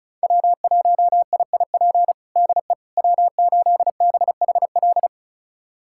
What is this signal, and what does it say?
Morse code, heard as a keyed tone, about 36 wpm, 700 Hz. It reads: W1IIP DE W8BHL
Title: USA Callsign DE Callsign - 5 Characters - 35wpm